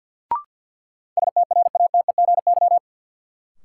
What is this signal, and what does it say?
Morse code, heard as a keyed tone, about 51 wpm, 700 Hz. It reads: STRATEGY
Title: Sentences from Top 1000 Words - Review Part 5 - New Word Drill - 50wpm